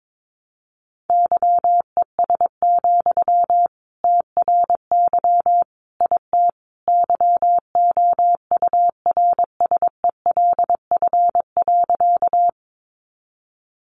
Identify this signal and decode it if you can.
Morse code, heard as a keyed tone, about 22 wpm, 700 Hz. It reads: YES, TRY IT YOURSELF.
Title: Sentences from Top 600 Words 22wpm